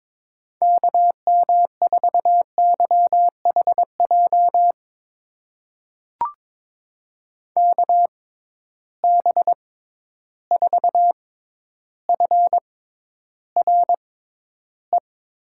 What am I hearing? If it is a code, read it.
Morse code, heard as a keyed tone, about 23 wpm, spaced as for 11 wpm, 700 Hz. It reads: KM4YHJ KB4FRE
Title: USA Call Signs - 6 Characters - 6x Character Spacing - 22wpm